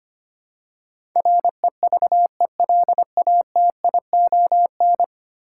Morse code, heard as a keyed tone, about 25 wpm, 690 Hz. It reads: REVELATION
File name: 10 Letter Words - First 3 Letters Given - 25wpm